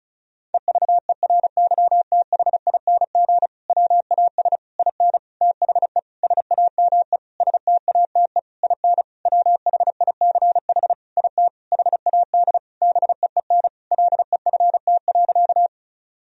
Morse code, heard as a keyed tone, about 36 wpm, 700 Hz. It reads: EVERYTHING WAS IN THE SAME STATE IN WHICH IT HAD BEEN LEFT.
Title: Sentences from Top 1000 Words - Review Part 2 - 35wpm